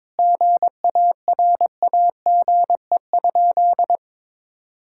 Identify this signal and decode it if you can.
Morse code, heard as a keyed tone, about 22 wpm, 700 Hz. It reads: GARAGE?
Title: Sentences from Top 1600 Words 22wpm